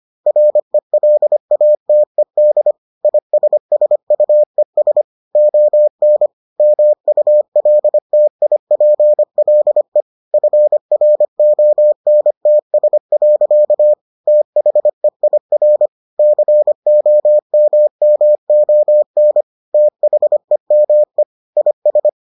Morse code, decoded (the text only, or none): RELATED ISSUES ON MULTIPLE FRONTS. THEIR COMMON THEME IS